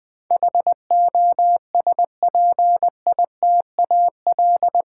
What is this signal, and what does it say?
Morse code, heard as a keyed tone, about 20 wpm, 700 Hz. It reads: HOSPITAL
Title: Eight Letter Words - 20wpm